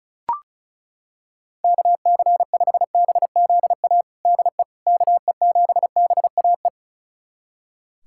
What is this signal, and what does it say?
Morse code, heard as a keyed tone, about 36 wpm, 700 Hz. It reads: KC5BZA DE KE7BAE
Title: USA Callsign DE Callsign 35wpm